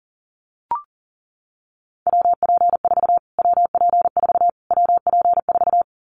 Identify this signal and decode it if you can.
Morse code, heard as a keyed tone, about 40 wpm, 700 Hz. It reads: WP4 WP4 WP4
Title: Callsign Prefixes for Top 25 DXCC - Repeated 3x - 40wpm